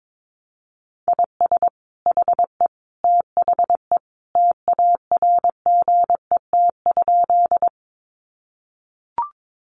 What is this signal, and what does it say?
Morse code, heard as a keyed tone, about 22 wpm, 700 Hz. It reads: IS HE THE TARGET?
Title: Sentences from Top 1200 Words 22wpm